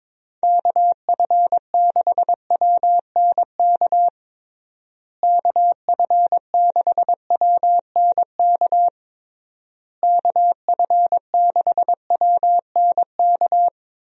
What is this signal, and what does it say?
Morse code, heard as a keyed tone, about 22 wpm, 700 Hz. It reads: KF6WNK KF6WNK KF6WNK
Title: USA Call Signs - 6 Characters - Repeated 3x - 22wpm